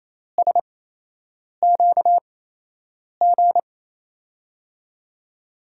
Morse code, heard as a keyed tone, about 28 wpm, spaced as for 8 wpm, 700 Hz. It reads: SQG
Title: USA Call Signs - 6 Characters - 8x Character Spacing - 28wpm